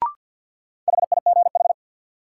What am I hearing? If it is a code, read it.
Morse code, heard as a keyed tone, about 47 wpm, 700 Hz. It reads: HIGH
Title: Sentences from Top 200 Words - New Word Drill - 50wpm